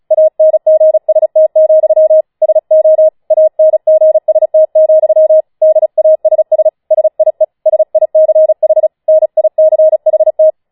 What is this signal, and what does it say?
Morse code, heard as a keyed tone, about 36 wpm, 630 Hz. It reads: ANGST, SO ANGST, DASS SIE SICH NICHT
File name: Schneewittchen Teil6 36wpm